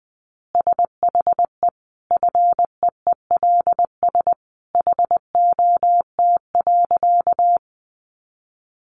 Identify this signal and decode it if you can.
Morse code, heard as a keyed tone, about 20 wpm, 700 Hz. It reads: SHE FEELS HOT.
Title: Easy Phrases 20wpm